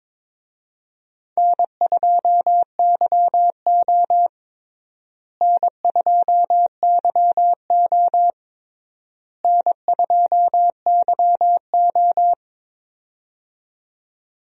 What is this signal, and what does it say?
Morse code, heard as a keyed tone, about 22 wpm, 700 Hz. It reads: N2YO N2YO N2YO
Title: USA Call Signs - 4 Characters - Repeated 3x - 22wpm